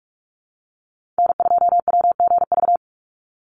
Morse code, heard as a keyed tone, about 44 wpm, 700 Hz. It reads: N2WGV
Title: USA Call Signs 45wpm